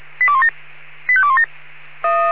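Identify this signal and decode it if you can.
Five-tone selective call (ZVEI), heard as quick runs of five tones, tones of about 70 ms. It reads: 85217
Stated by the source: ZVEI five-tone selective call received by SDR, code 85217.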